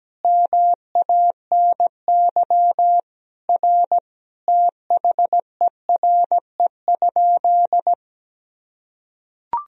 Morse code, heard as a keyed tone, about 17 wpm, 700 Hz. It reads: MANY R THERE?
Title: Sentences from Top 100 Words and Abbreviations 17wpm